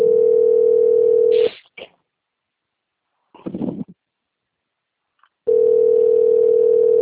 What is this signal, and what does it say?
Ringback tone (North America), 440 + 480 Hz.
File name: out- - - - - .36766